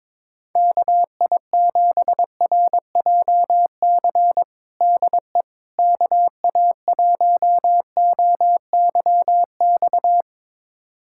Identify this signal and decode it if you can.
Morse code, heard as a keyed tone, about 22 wpm, 700 Hz. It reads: KI7RJC DE KA1OYX
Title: USA Callsign DE Callsign - 6 Characters - 22wpm